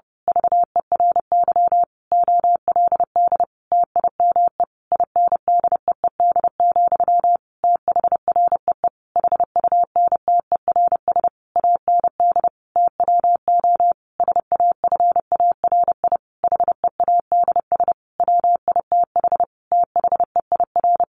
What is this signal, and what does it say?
Morse code, heard as a keyed tone, about 30 wpm, 700 Hz. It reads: VERY OLD TIME INDEED, THREE HUNTERS AND TWO SAFARI HEADS WITH THEIR
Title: A Sound Of Thunder 30wpm